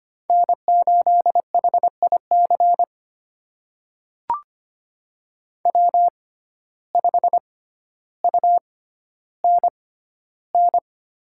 Morse code, heard as a keyed tone, about 25 wpm, spaced as for 13 wpm, 700 Hz. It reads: N8HIC W5UNN
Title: USA Call Signs - 5 Characters - 6x Character Spacing - 25wpm